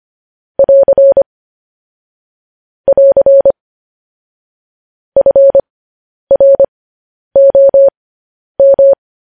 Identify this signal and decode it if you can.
Morse code, heard as a keyed tone, about 25 wpm, spaced as for 10 wpm, 550 Hz. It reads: <AR> <AR> FROM
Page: news-headlines-in-morse-code-at-10-wpm